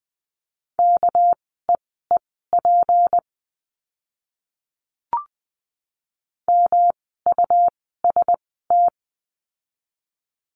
Morse code, heard as a keyed tone, about 20 wpm, spaced as for 15 wpm, 700 Hz. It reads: KEEP MUST
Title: 237 Lesson 38f - Letter q - Words 20wpm